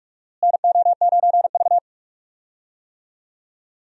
Morse code, heard as a keyed tone, about 42 wpm, 700 Hz. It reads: NO9V
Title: USA Call Signs - 4 Characters - Repeated 2x - 45wpm